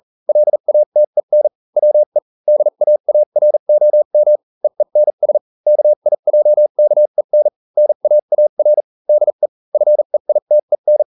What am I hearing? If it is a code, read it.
Morse code, heard as a keyed tone, about 40 wpm, spaced as for 35 wpm, 600 Hz. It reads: LATEN WE DAAROM EENS KIJKEN NAAR DE FEITEN